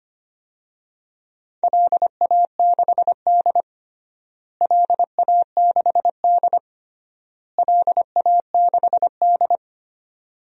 Morse code, heard as a keyed tone, about 25 wpm, 700 Hz. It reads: LA6D LA6D LA6D
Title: DXCC-100 Call Signs - 4 Characters - Repeated 3x - 25wpm